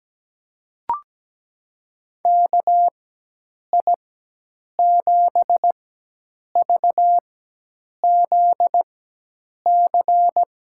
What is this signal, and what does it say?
Morse code, heard as a keyed tone, about 17 wpm, spaced as for 8 wpm, 700 Hz. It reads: KI7VZC
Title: USA Call Signs - 6 Characters - 4x Character Spacing - 17wpm